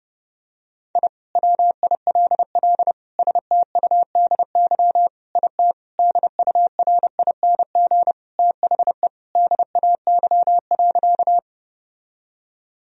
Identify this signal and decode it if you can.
Morse code, heard as a keyed tone, about 30 wpm, 700 Hz. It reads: I WILL STUDY IT DURING THE DAY.
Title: Sentences from Top 1000 Words - Review Part 2 - 30wpm